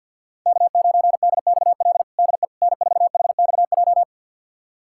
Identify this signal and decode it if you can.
Morse code, heard as a keyed tone, about 52 wpm, 700 Hz. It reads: K9DXL DE N4HXJ
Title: USA Callsign DE Callsign - 5 Characters - 50wpm